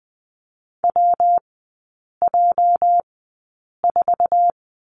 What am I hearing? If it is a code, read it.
Morse code, heard as a keyed tone, about 20 wpm, spaced as for 8 wpm, 700 Hz. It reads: WJ4
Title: 205 Lesson 33e - Letter j - Three Characters 20wpm